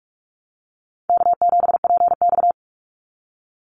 Morse code, heard as a keyed tone, about 44 wpm, 700 Hz. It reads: K7PX
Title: NA SOTA Activator Callsigns 45wpm